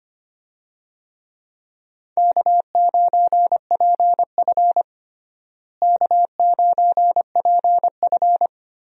Morse code, heard as a keyed tone, about 25 wpm, 700 Hz. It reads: K9PF K9PF
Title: USA Call Signs - 4 Characters - Repeated 3x - 25wpm